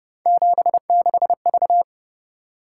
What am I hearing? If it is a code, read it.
Morse code, heard as a keyed tone, about 31 wpm, 700 Hz. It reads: 76V
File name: Speed-Racing - Three Letter-Number Combinations - 25wpm